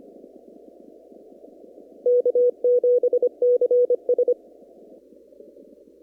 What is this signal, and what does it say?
Morse code, heard as a keyed tone, about 25 wpm, 490 Hz. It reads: K7CS
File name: CW / kt5j wpx cw-op-k5pi